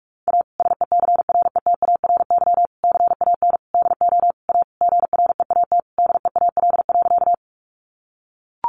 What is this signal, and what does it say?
Morse code, heard as a keyed tone, about 44 wpm, 700 Hz. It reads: A SECRETARY CAN DO A GREAT DEAL.
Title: Sentences from Top 1300 Words 45wpm